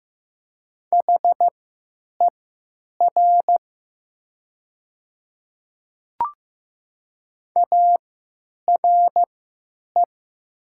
Morse code, heard as a keyed tone, about 15 wpm, spaced as for 9 wpm, 690 Hz. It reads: HER ARE
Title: 3x Character Spacing - Top 100 Words - 15wpm